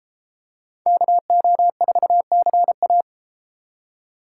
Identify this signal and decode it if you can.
Morse code, heard as a keyed tone, about 32 wpm, 700 Hz. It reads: KO4CA